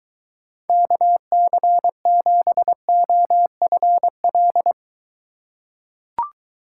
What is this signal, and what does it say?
Morse code, heard as a keyed tone, about 22 wpm, 700 Hz. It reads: KC7OFL